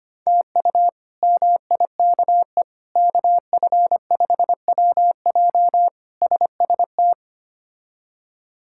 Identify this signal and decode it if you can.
Morse code, heard as a keyed tone, about 25 wpm, 700 Hz. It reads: TU MIKE KF5WJ SST
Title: K1USN SST QSO Practice - 25wpm